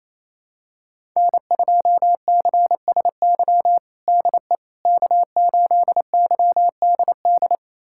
Morse code, heard as a keyed tone, about 28 wpm, 700 Hz. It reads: N2CSY DE K8YDD
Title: USA Callsign DE Callsign - 5 Characters - 28wpm